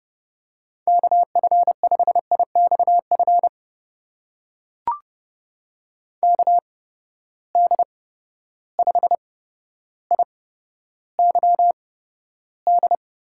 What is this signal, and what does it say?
Morse code, heard as a keyed tone, about 30 wpm, spaced as for 13 wpm, 700 Hz. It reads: KF5IXF KD5IYD
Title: USA Call Signs - 6 Characters - 8x Character Spacing - 30wpm